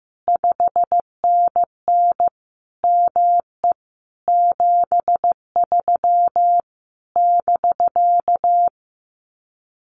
Morse code, heard as a keyed tone, about 14 wpm, 700 Hz. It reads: 5NN ME 73 <BK>
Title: Common QSO Phrases Encore 15wpm